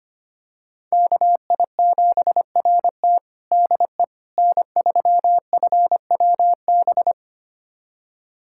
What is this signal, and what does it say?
Morse code, heard as a keyed tone, about 25 wpm, 700 Hz. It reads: KI7RT DE N3FWB
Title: USA Callsign DE Callsign - 5 Characters - 25wpm